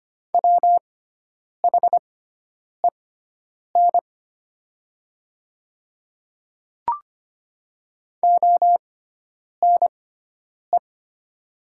Morse code, heard as a keyed tone, about 25 wpm, spaced as for 9 wpm, 700 Hz. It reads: WHEN ONE
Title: 6x Character Spacing - Top 200 Words - 25wpm